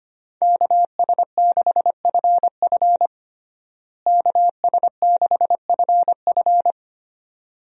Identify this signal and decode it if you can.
Morse code, heard as a keyed tone, about 25 wpm, 700 Hz. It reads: KS6FF KS6FF
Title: USA Call Signs - 5 Characters - Repeated 2x - 25wpm